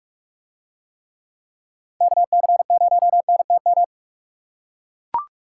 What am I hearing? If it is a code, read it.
Morse code, heard as a keyed tone, about 41 wpm, 700 Hz. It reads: KC0NTM